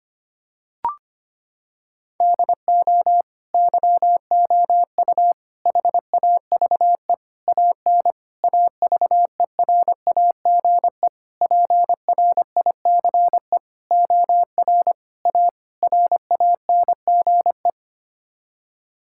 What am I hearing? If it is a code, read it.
Morse code, heard as a keyed tone, about 25 wpm, 700 Hz. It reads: DO YOU HAVE AN AVERAGE PRICE OR A RANGE
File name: Sets of 9 Words - 25wpm